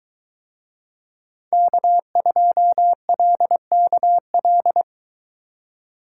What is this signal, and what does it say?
Morse code, heard as a keyed tone, about 22 wpm, 700 Hz. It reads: K2LKL